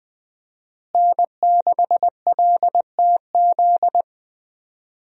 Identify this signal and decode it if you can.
Morse code, heard as a keyed tone, about 20 wpm, 700 Hz. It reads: N6LTZ
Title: USA Call Signs - 5 Characters - Repeated 3x - 20wpm